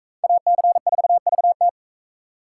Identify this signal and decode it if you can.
Morse code, heard as a keyed tone, about 42 wpm, 700 Hz. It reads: AC4VT